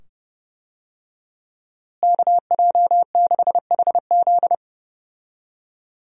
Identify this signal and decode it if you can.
Morse code, heard as a keyed tone, about 30 wpm, 700 Hz. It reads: KJ6HZ
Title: USA Call Signs - 5 Characters - Repeated 2x - 30wpm